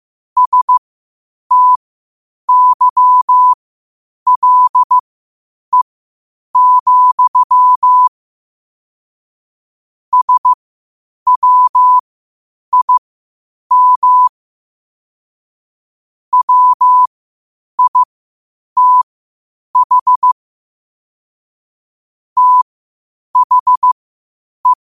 Morse code, heard as a keyed tone, about 15 wpm, spaced as for 8 wpm, 1000 Hz. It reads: STYLE, SWIM WITH THE
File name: qotdfiles / 8wpm